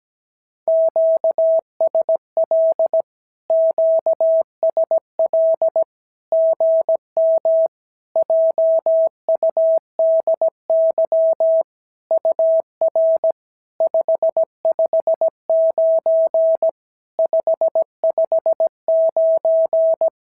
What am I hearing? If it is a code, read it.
Morse code, heard as a keyed tone, about 17 wpm, 650 Hz. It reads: QSL QSL GM JUDY UR 559 559
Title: SOTA QSO Practice 17wpm